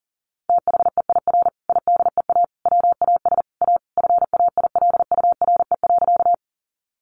Morse code, heard as a keyed tone, about 40 wpm, 700 Hz. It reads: THEIR IDEA WAS A FAILURE.
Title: Sentences from Top 900 Words 40wpm